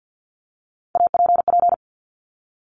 Morse code, heard as a keyed tone, about 50 wpm, 700 Hz. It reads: APP
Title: Top Three Letter Combinations in Words 50wpm